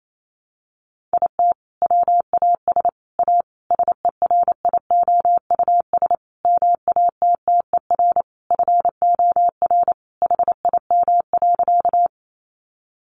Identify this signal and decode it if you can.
Morse code, heard as a keyed tone, about 28 wpm, 700 Hz. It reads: IT WAS A SERIOUS MATTER FOR HIM.
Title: Sentences from Top 1100 Words 28wpm